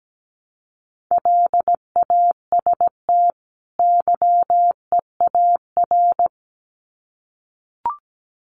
Morse code, heard as a keyed tone, about 17 wpm, 700 Hz. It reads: LAST YEAR
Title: Sets of 2 Words 17wpm